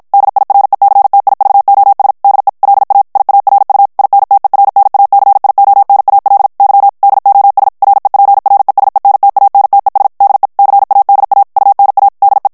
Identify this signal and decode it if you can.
Morse code, heard as a keyed tone, about 55 wpm, 800 Hz. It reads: DIRECTIVOS DE LA IARU INTERNACIONAL Y DOS REPRESENTANTES DE CADA UNA DE
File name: ESPAGNOL TEXTE EN CLAIR 1 VITESSE 50 MOTS MINUTES